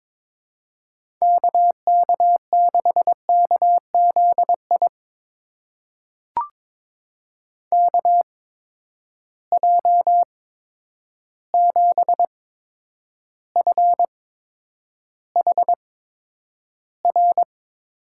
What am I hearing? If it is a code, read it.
Morse code, heard as a keyed tone, about 22 wpm, spaced as for 9 wpm, 700 Hz. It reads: KK6KZI KJ7FHR
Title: USA Call Signs - 6 Characters - 8x Character Spacing - 22wpm